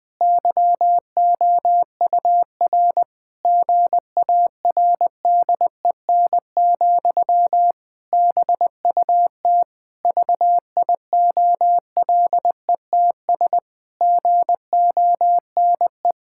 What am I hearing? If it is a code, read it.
Morse code, heard as a keyed tone, about 20 wpm, 700 Hz. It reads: YOUR GARDEN, BUT VIOLETS GONE
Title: Pangram Challenge v2 - 20wpm